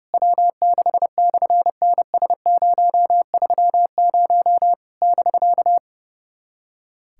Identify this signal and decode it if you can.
Morse code, heard as a keyed tone, about 30 wpm, 700 Hz. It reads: W6/NS030 <BK>
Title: SOTA QSO Practice 30wpm